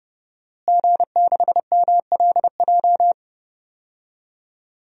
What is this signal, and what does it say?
Morse code, heard as a keyed tone, about 30 wpm, 700 Hz. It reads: G6MLJ
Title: DXCC-100 Call Signs - 5 Characters - Repeated 2x - 30wpm